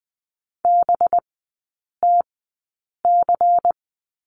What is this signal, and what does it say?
Morse code, heard as a keyed tone, about 20 wpm, spaced as for 8 wpm, 700 Hz. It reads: BTC
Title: 184 Lesson 30e - Letter b - Three Characters 20wpm